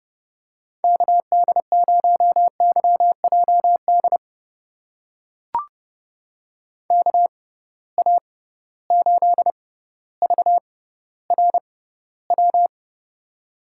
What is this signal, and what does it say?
Morse code, heard as a keyed tone, about 30 wpm, spaced as for 15 wpm, 700 Hz. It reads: KD0YJD KA8VRW
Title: USA Call Signs - 6x Character Spacing - 30wpm